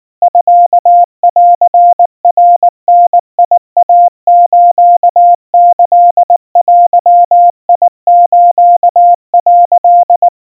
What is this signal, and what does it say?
Morse code (Japanese Wabun code), heard as a keyed tone, about 19 wpm, 700 Hz. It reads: ミンナダイスキデス」
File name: kawa / Morse sample / kamyu 70 cpm 700 Hz Crispy